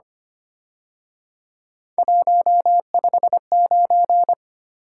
Morse code, heard as a keyed tone, about 29 wpm, 700 Hz. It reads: 159
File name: Speed-Racing - Three Letter-Number Combinations - 25wpm